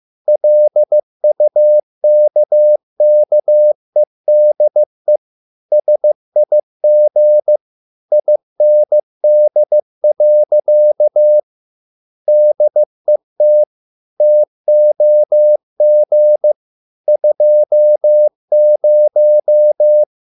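Morse code, heard as a keyed tone, about 15 wpm, 600 Hz. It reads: LUKKEDE SIG IND. DET TOG 20
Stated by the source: Morse code at a slow 15 wpm from CW med Gnister.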